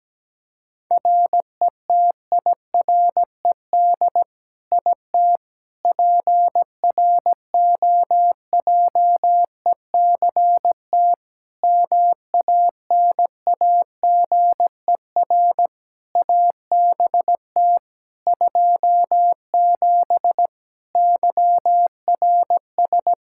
Morse code, heard as a keyed tone, about 17 wpm, 700 Hz. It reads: RETIRED IT PROJECT MANAGER ABT 27 YRS
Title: Practice QSO - Exchange 3 - More Info - 17wpm